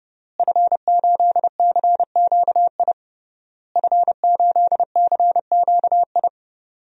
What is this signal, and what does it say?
Morse code, heard as a keyed tone, about 30 wpm, 700 Hz. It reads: F8CQI F8CQI
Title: DXCC-100 Call Signs - 5 Characters - Repeated 3x - 30wpm